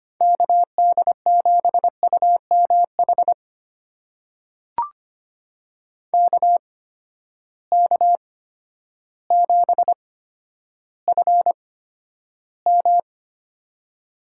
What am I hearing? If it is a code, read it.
Morse code, heard as a keyed tone, about 25 wpm, spaced as for 11 wpm, 700 Hz. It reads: KD7UMH KK7FM
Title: USA Call Signs - 8x Character Spacing - 25wpm